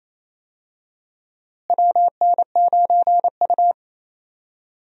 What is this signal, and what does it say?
Morse code, heard as a keyed tone, about 28 wpm, 700 Hz. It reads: WN9U
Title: USA Call Signs - 4 Characters - Repeated 2x - 28wpm